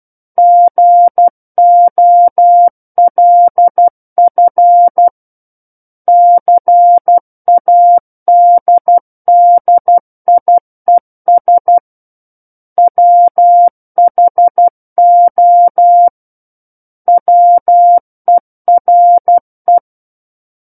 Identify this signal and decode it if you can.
Morse code, heard as a keyed tone, about 12 wpm, 700 Hz. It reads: GOLF CADDIES WHO WERE
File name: Book One / 03-Chapter 3 - 12 WPM